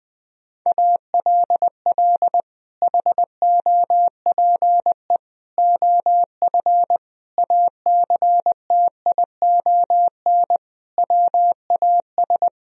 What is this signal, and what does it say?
Morse code, heard as a keyed tone, about 21 wpm, 700 Hz. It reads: ALL HOPE OF ACTION WAS
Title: Sentences from Top 1000 Words - Review Part 4 - 20wpm